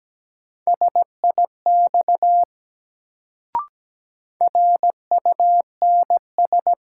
Morse code, heard as a keyed tone, about 17 wpm, 700 Hz. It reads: SIX RUNS
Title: Sentences from Top 600 Words - New Word Drill - 17wpm